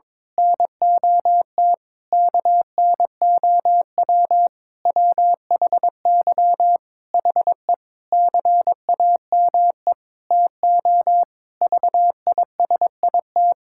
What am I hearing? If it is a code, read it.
Morse code, heard as a keyed tone, about 22 wpm, 700 Hz. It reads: NOT KNOW WHY HE CAME TO VISIT
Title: Sentences from Top 1000 Words - Review Part 6 - 22wpm